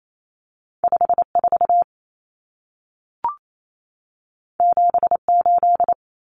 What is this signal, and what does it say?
Morse code, heard as a keyed tone, about 28 wpm, 700 Hz. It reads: 54 78
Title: Numbers - 2-Digit 28wpm